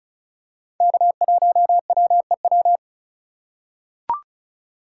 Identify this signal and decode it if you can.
Morse code, heard as a keyed tone, about 36 wpm, 700 Hz. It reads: K1WEW
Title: USA Call Signs - 5 Characters - 2x Character Spacing - 35wpm